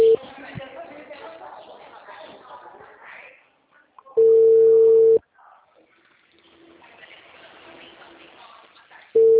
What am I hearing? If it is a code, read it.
Ringback tone (China, 450 Hz), with speech.
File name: out- -2029- - - .64922